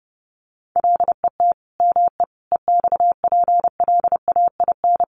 Morse code, heard as a keyed tone, about 28 wpm, 700 Hz. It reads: LET ME EXPLAIN
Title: Sentences from Top 1300 Words 30wpm